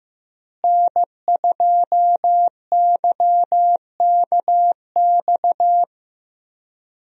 Morse code, heard as a keyed tone, about 15 wpm, 700 Hz. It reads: N2YKX
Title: USA Call Signs - 5 Characters - Repeated 2x - 15wpm